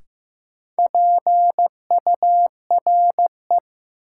Morse code, heard as a keyed tone, about 15 wpm, 700 Hz. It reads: PURE
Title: Sentences from Top 1500 Words - New Word Drill - 15wpm